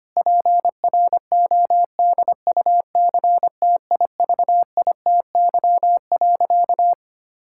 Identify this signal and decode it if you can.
Morse code, heard as a keyed tone, about 25 wpm, 700 Hz. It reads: PRODUCTIVITY.
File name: practice / 25wpm / Difficult Sentences with Hints - 25wpm